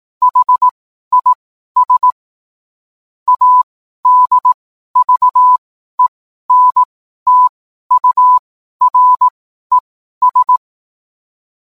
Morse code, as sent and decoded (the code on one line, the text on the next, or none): .... .. ... / .- -.. ...- . -. - ..- .-. ...
HIS ADVENTURES